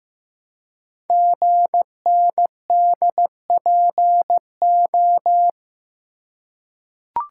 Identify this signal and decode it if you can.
Morse code, heard as a keyed tone, about 15 wpm, 700 Hz. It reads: GNDPO
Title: Code Groups - 5 Letters - 15wpm